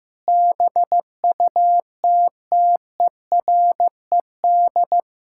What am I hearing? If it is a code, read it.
Morse code, heard as a keyed tone, about 14 wpm, 700 Hz. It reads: BUTTERED
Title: Pangram Challenge v2 - 15wpm